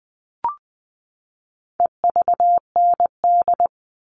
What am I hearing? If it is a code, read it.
Morse code, heard as a keyed tone, about 20 wpm, 700 Hz. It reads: EVND
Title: All Cut Numbers - 4-Digit 20wpm